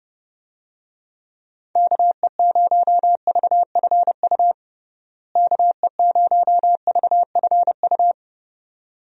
Morse code, heard as a keyed tone, about 31 wpm, 700 Hz. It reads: KE0VFU KE0VFU
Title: USA Call Signs - 6 Characters - Repeated 2x - 30wpm